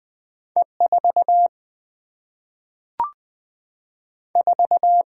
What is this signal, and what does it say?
Morse code, heard as a keyed tone, about 20 wpm, 700 Hz. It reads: E 4 4
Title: Single-Letter Single-Number 20wpm